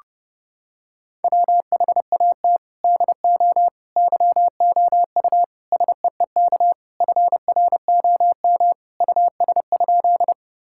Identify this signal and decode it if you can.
Morse code, heard as a keyed tone, about 30 wpm, 700 Hz. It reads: WHAT DO YOU SEEK FROM US?
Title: Sentences from Top 1600 Words 30wpm